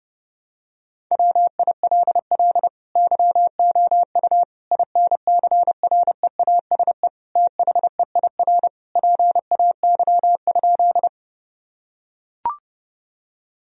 Morse code, heard as a keyed tone, about 30 wpm, 700 Hz. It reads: WILL YOU INCREASE THEIR PAY?
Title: Sentences from Top 1000 Words - Review Part 4 - 30wpm